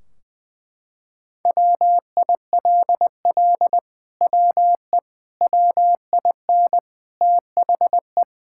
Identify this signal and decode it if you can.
Morse code, heard as a keyed tone, about 20 wpm, 700 Hz. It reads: WILL WE WIN THE
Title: Sentences from Top 1600 Words 20wpm